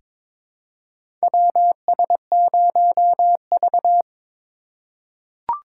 Morse code, heard as a keyed tone, about 22 wpm, 700 Hz. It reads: WS0V
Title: USA Call Signs - 4 Characters - 8x Character Spacing - 22wpm